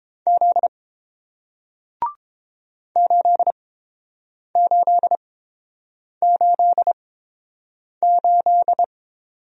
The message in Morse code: Z 8888

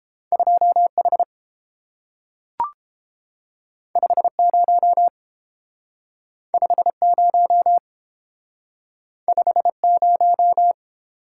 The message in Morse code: 2H 50 50 50

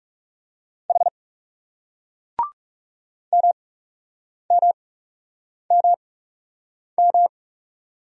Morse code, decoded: H MMMM

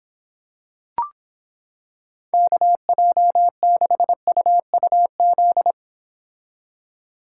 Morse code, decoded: KJ6UUZ

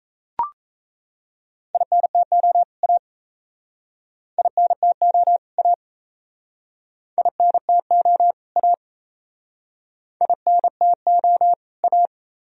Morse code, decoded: INTO A INTO A INTO A INTO A

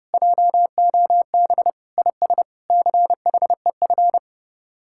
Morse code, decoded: JOB IS CHEF

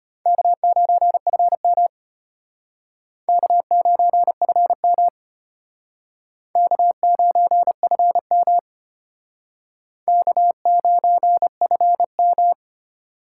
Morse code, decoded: K9FM K9FM K9FM K9FM

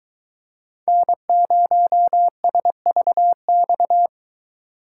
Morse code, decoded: N0SVX

